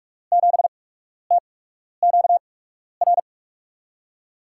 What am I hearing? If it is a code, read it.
Morse code, heard as a keyed tone, about 46 wpm, spaced as for 12 wpm, 700 Hz. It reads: 7TQR
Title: USA Call Signs - 8x Character Spacing - 45wpm